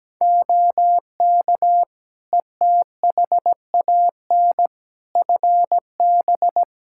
Morse code, heard as a keyed tone, about 17 wpm, 700 Hz. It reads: OK ETHAN FB
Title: Practice QSO - Exchange 2 - Info - 17wpm